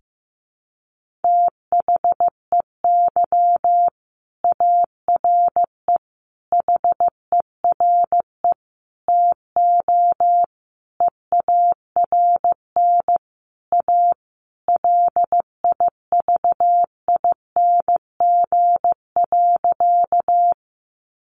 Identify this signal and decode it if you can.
Morse code, heard as a keyed tone, about 15 wpm, 700 Hz. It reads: THEY ARE HERE TO EARN A LIVING.
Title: Sentences from Top 1500 Words 15wpm